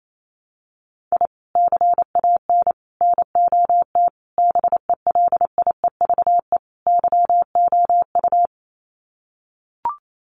Morse code, heard as a keyed tone, about 28 wpm, 700 Hz. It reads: I CAN NOT BELIEVE YOU
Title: Sets of 5 Words Encore 28wpm